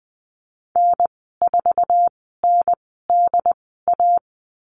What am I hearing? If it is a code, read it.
Morse code, heard as a keyed tone, about 20 wpm, spaced as for 15 wpm, 700 Hz. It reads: N4NDA
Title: 076 Lesson 13g - Letter L - Call Signs 20wpm